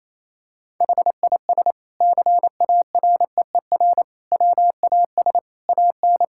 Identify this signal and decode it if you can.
Morse code, heard as a keyed tone, about 27 wpm, 700 Hz. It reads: HIS CAREER WAS AN
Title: Sentences from Top 1000 Words - Review Part 6 - 28wpm